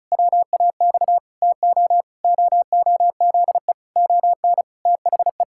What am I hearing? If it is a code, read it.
Morse code, heard as a keyed tone, about 36 wpm, 700 Hz. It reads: WAX TO OOZE ON THE
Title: Pangram Challenge v2 - 35wpm